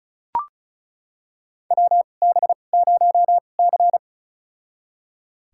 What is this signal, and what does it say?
Morse code, heard as a keyed tone, about 36 wpm, spaced as for 25 wpm, 700 Hz. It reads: WB0C
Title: USA Call Signs - 4 Characters - 2x Character Spacing - 35wpm